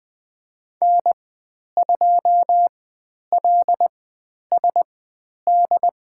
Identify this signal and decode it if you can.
Morse code, heard as a keyed tone, about 20 wpm, spaced as for 10 wpm, 700 Hz. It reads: N2LSD
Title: 095 Lesson 16g - Letter c - Call Signs 20wpm